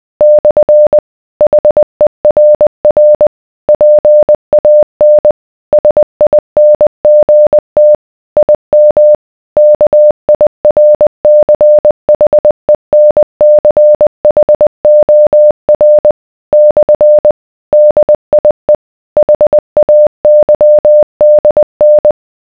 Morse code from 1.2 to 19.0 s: HERR PAN SINGT IM KIRCHENCHOR / DIE